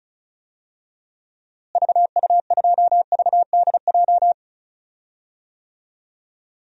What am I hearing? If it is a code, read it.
Morse code, heard as a keyed tone, about 34 wpm, 700 Hz. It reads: VU2VDJ
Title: DXCC-100 Call Signs - 6 Characters - Repeated 2x - 35wpm